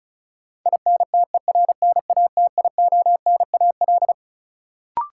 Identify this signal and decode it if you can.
Morse code, heard as a keyed tone, about 34 wpm, 700 Hz. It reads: INTERNATIONAL
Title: Sentences from Top 1000 Words - Review Part 4 - New Word Drill - 35wpm